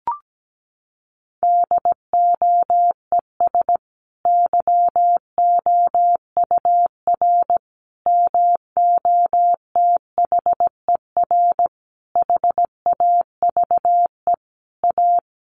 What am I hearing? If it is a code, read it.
Morse code, heard as a keyed tone, about 17 wpm, 700 Hz. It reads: DOES YOUR MOTHER HAVE A
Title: Sentences from Top 1400 Words 17wpm